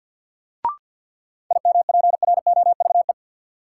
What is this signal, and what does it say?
Morse code, heard as a keyed tone, about 47 wpm, 700 Hz. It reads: IMPROVE
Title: Sentences from Top 800 Words - New Word Drill - 50wpm